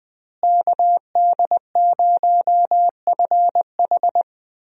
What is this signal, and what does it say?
Morse code, heard as a keyed tone, about 21 wpm, 700 Hz. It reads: KD0FH
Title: USA Call Signs - 5 Characters - Repeated 2x - 20wpm